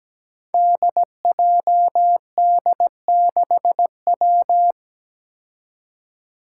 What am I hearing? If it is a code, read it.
Morse code, heard as a keyed tone, about 17 wpm, 700 Hz. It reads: DJD6W
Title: Five Challenging Letter-Number Combinations 17wpm